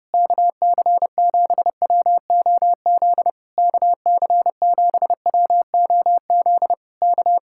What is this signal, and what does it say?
Morse code, heard as a keyed tone, about 32 wpm, 700 Hz. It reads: KC7WOZ KC7WOZ K